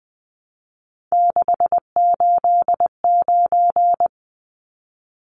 Morse code, heard as a keyed tone, about 20 wpm, 700 Hz. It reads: 689
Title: Numbers - 3-Digit 20wpm